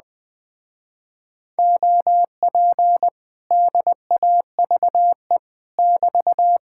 Morse code, heard as a keyed tone, about 20 wpm, 700 Hz. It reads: OP DAVE <BT>
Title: Speed-Racing Common QSO Phrases Encore - 17wpm